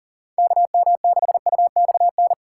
Morse code, heard as a keyed tone, about 40 wpm, 700 Hz. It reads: KM6UXN
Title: USA Call Signs - 6 Characters - 3x Character Spacing - 40wpm